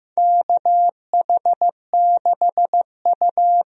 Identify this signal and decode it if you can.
Morse code, heard as a keyed tone, about 15 wpm, 700 Hz. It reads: KH6U
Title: DXCC-100 Call Signs - 4 Characters - 15wpm